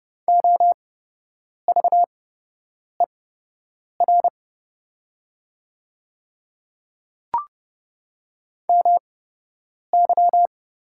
Morse code, heard as a keyed tone, about 30 wpm, spaced as for 8 wpm, 700 Hz. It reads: OVER MY
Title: 8x Character Spacing - Top 100 Words - 30wpm